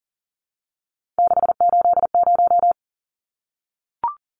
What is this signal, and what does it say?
Morse code, heard as a keyed tone, about 40 wpm, 700 Hz. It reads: 680
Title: Numbers - 3-Digit 40wpm